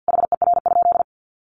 Morse code, heard as a keyed tone, about 50 wpm, 700 Hz. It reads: HER?
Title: Sentences from Top 1400 Words 50wpm